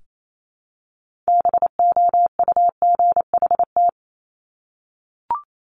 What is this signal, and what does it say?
Morse code, heard as a keyed tone, about 28 wpm, 700 Hz. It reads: BOUGHT
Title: Multi-letter Phonemes Repeated 3x 28wpm